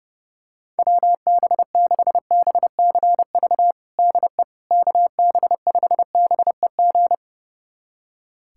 Morse code, heard as a keyed tone, about 30 wpm, 700 Hz. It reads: WB6BCV DE KB5BEG
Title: USA Callsign DE Callsign - 6 Characters - 30wpm